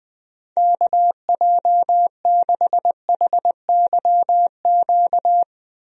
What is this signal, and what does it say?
Morse code, heard as a keyed tone, about 20 wpm, 700 Hz. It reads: KJ6HYQ
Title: USA Call Signs - 6 Characters - Repeated 2x - 20wpm